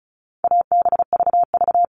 Morse code, heard as a keyed tone, about 34 wpm, 700 Hz. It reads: ABVV
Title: All Cut Numbers - 4-Digit 35wpm